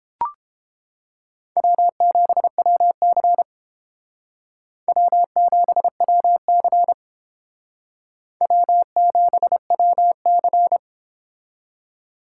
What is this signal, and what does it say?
Morse code, heard as a keyed tone, about 31 wpm, 700 Hz. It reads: W7WC W7WC W7WC